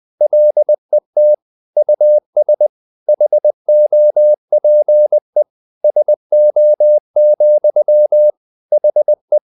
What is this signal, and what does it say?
Morse code, heard as a keyed tone, about 20 wpm, 600 Hz. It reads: LET US HOPE SO, HE
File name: the old man and the sea / the old man and the sea part 1 20wpm